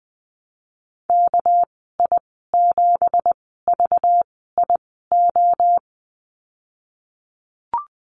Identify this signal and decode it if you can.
Morse code, heard as a keyed tone, about 20 wpm, spaced as for 15 wpm, 700 Hz. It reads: KI7VIO
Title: 207 Lesson 33g - Letter j - Call Signs 20wpm